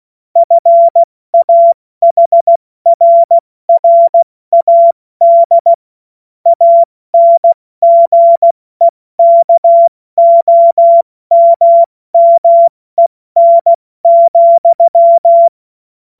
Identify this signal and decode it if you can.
Morse code, heard as a keyed tone, about 16 wpm, spaced as for 14 wpm, 680 Hz. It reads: FAHRRAD ANGEKOMMEN,
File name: Morsekurs / Witze / Witz 27 Mai 2025 70 F80